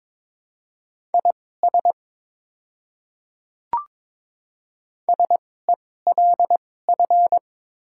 Morse code, heard as a keyed tone, about 22 wpm, spaced as for 16 wpm, 700 Hz. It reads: IS SELF
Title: 2x Character Spacing - Top 300 Words - 22wpm